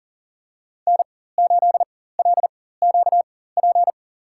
Morse code, heard as a keyed tone, about 40 wpm, spaced as for 19 wpm, 700 Hz. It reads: N8LQP
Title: USA Call Signs - 5 Characters - 4x Character Spacing - 40wpm